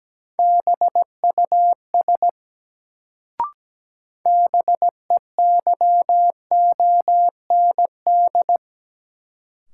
Morse code, heard as a keyed tone, about 17 wpm, 700 Hz. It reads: BUS BEYOND
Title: Sentences from Top 1000 Words - Review Part 6 - New Word Drill - 17wpm